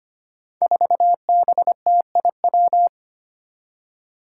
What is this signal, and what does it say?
Morse code, heard as a keyed tone, about 25 wpm, 700 Hz. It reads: 4BTIW
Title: Five Challenging Letter-Number Combinations 25wpm